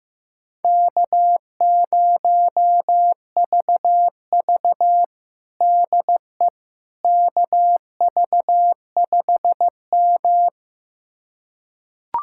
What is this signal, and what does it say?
Morse code, heard as a keyed tone, about 15 wpm, 700 Hz. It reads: K0VV DE KV5M
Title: USA Callsign DE Callsign - 4 Characters - 15wpm